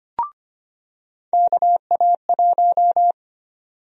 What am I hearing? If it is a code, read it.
Morse code, heard as a keyed tone, about 25 wpm, 700 Hz. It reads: KA1
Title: USA Call Sign Prefixes - 25wpm